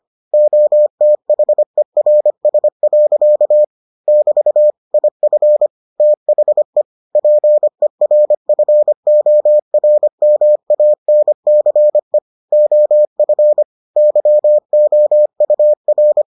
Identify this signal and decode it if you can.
Morse code, heard as a keyed tone, about 25 wpm, 600 Hz. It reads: OTHERS. <BT> IF THE PERFORMANCE OF YOUR